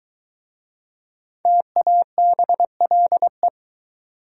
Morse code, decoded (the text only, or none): TABLE